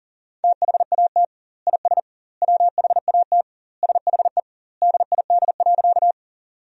Morse code, heard as a keyed tone, about 37 wpm, 700 Hz. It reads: THAT IS WHAT SHE DID.
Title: 2x Word-Spacing - Sentences from Top 100 Words - 40wpm